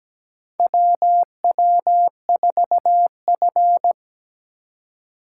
Morse code, heard as a keyed tone, about 17 wpm, 700 Hz. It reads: WW4F
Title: USA Call Signs - 4 Characters - 3x Character Spacing - 17wpm